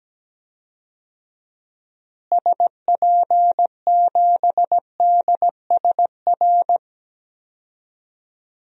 Morse code, heard as a keyed tone, about 17 wpm, 700 Hz. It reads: SP7DSR
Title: DXCC-100 Call Signs - 6 Characters - Repeated 3x - 17wpm